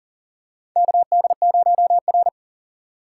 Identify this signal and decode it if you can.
Morse code, heard as a keyed tone, about 37 wpm, 700 Hz. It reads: KD0R
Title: USA Call Signs - 4 Characters - 6x Character Spacing - 40wpm